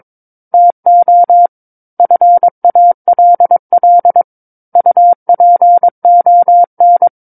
Morse code, heard as a keyed tone, about 22 wpm, 700 Hz. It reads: TO FALL UPON
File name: Book One / 13-Chapter 13 - 22 WPM